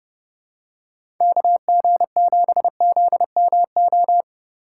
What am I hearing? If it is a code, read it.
Morse code, heard as a keyed tone, about 30 wpm, 700 Hz. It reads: KG7ZMO
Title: USA Call Signs - 6 Characters - 8x Character Spacing - 30wpm